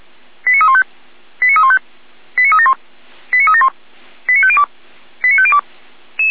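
Five-tone selective call (ZVEI), heard as quick runs of five tones, tones of about 70 ms. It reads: 89315 89315 89371 89371 89502 89502
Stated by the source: ZVEI five-tone selective call received by SDR, from fire brigade Feuerwehr Orlamünde.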